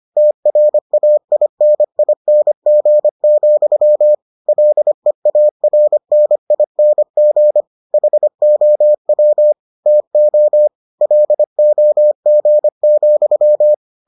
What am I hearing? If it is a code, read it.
Morse code, heard as a keyed tone, about 25 wpm, 600 Hz. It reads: TRAINING, LEARNING HOW TO LOG,